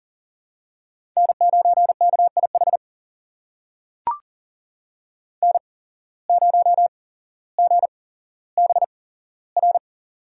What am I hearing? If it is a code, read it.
Morse code, heard as a keyed tone, about 40 wpm, spaced as for 17 wpm, 700 Hz. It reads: N9KIH N0GBR
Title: USA Call Signs - 5 Characters - 8x Character Spacing - 40wpm